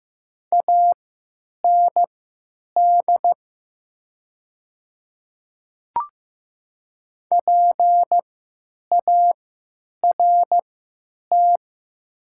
Morse code, heard as a keyed tone, about 15 wpm, spaced as for 9 wpm, 700 Hz. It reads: AND PART
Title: 3x Character Spacing - Top 300 Words - 15wpm